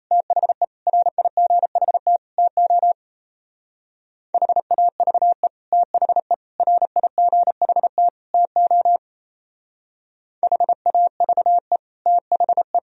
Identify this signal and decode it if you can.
Morse code, heard as a keyed tone, about 35 wpm, 700 Hz. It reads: THE RIGHT TO HAVE THE RIGHT TO HAVE THE
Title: Speed-Racing - Sets of 4 Words 28wpm